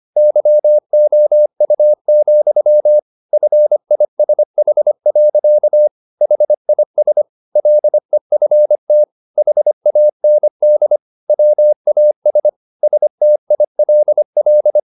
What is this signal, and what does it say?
Morse code, heard as a keyed tone, about 25 wpm, 600 Hz. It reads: YOU, FISH. HIS LEFT HAND WAS STILL
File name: the old man and the sea part 1 25wpm